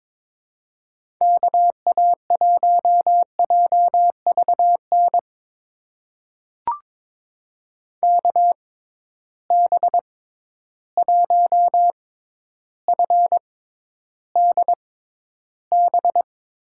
Morse code, heard as a keyed tone, about 22 wpm, spaced as for 11 wpm, 700 Hz. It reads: KA1JVN KB1FDB
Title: USA Call Signs - 6 Characters - 6x Character Spacing - 22wpm